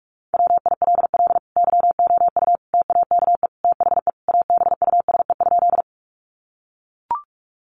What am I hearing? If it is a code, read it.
Morse code, heard as a keyed tone, about 44 wpm, 700 Hz. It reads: WILL YOU TAKE THE ABUSE?
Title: Sentences from Top 1300 Words 45wpm